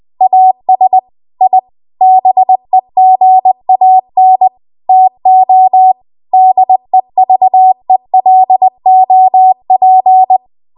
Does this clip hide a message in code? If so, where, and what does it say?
Yes: Morse code from 0.2 to 10.4 s, heard as a keyed tone, about 20 wpm, 750 Hz. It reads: AS I BEGAN TO DEVELOP